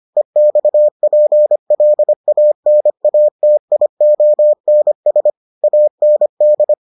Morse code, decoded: EXPLANATIONS AND

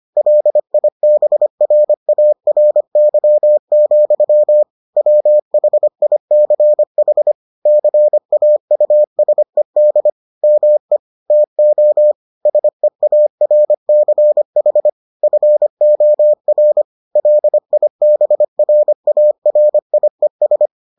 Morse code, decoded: LIBRARY, WHICH CAUSED ME TO SEARCH FOR LIBRARIES

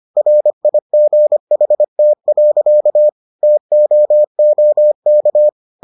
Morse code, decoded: RIGHT. TOOK

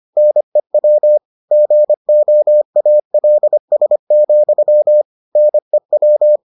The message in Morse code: NEW GOALS, NEW

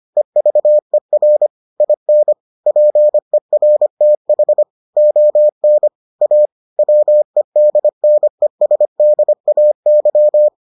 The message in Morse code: EVER IN PERTH ON A WEDNESDAY

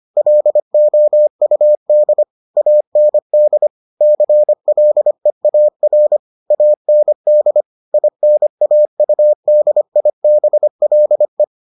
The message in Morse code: LOUD AND CLEAR AND INAUDIBLE

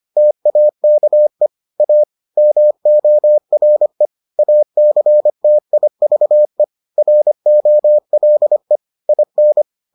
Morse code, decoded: TAKE A MORE ACTIVE ROLE IN

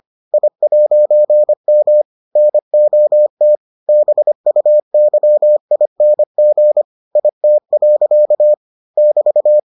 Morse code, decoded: I'M NOT BUYING IT. <BT>

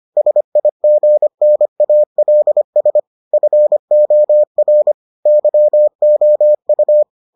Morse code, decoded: SIGNALS FOR YOU

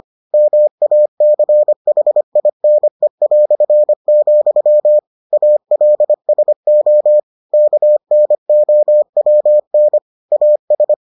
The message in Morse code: MACHINE", ALSO KNOWN AS

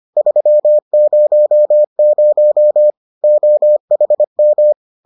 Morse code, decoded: 300 OHM

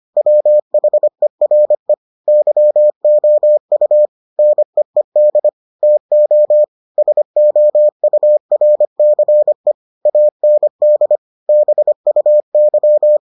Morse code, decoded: WHERE YOU NEED TO SOURCE AND BUY

